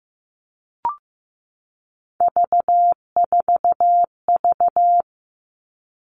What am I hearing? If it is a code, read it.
Morse code, heard as a keyed tone, about 15 wpm, 700 Hz. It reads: V4V
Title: All Cut Numbers - 3-Digit 15wpm